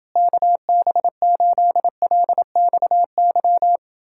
Morse code, heard as a keyed tone, about 29 wpm, 700 Hz. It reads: KB8LXY